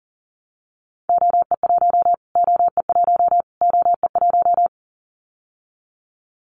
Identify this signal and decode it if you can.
Morse code, heard as a keyed tone, about 40 wpm, 700 Hz. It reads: OE1 OE1 OE1
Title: Callsign Prefixes for Top 25 DXCC - Repeated 3x - 40wpm